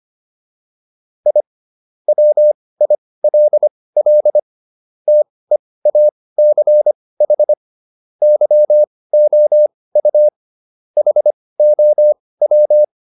Morse code, heard as a keyed tone, about 25 wpm, spaced as for 18 wpm, 600 Hz. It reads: I WILL TEACH YOU HOW